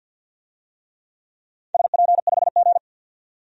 Morse code, heard as a keyed tone, about 53 wpm, 700 Hz. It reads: SP5G
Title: DXCC-100 Call Signs - 4 Characters - 50wpm